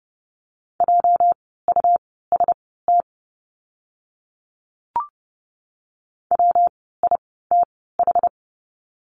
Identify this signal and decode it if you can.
Morse code, heard as a keyed tone, about 30 wpm, spaced as for 17 wpm, 700 Hz. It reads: JUST WITH
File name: course / 30:10wpm / 230 Lesson 37f - Letter x - Words 30wpm